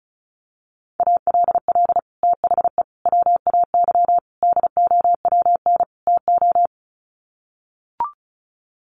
Morse code, heard as a keyed tone, about 34 wpm, 700 Hz. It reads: ALL THE WAY DOWN TO
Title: Sets of 5 Words Encore2 35wpm